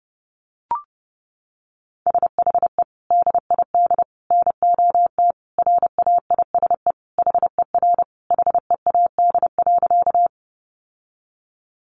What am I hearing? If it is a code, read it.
Morse code, heard as a keyed tone, about 30 wpm, 700 Hz. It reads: SHE DID NOT RAISE HER HEAD.
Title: Sentences from Top 1400 Words 30wpm